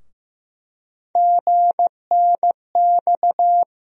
Morse code, heard as a keyed tone, about 15 wpm, 700 Hz. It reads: GNX
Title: USA Call Sign Postfixes - 3 Letters 15wpm